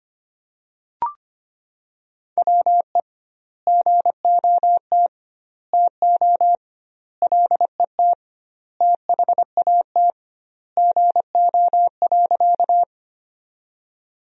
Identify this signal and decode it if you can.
Morse code, heard as a keyed tone, about 25 wpm, 700 Hz. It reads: WE GOT TO LET THAT GO.
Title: 2x Word-Spacing - Sentences from Top 300 Words - 25wpm